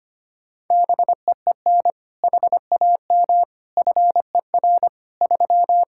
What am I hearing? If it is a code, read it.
Morse code, heard as a keyed tone, about 25 wpm, 700 Hz. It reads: BEEN HAM FER 3
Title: Practice QSO - Exchange 3 - More Info - 25wpm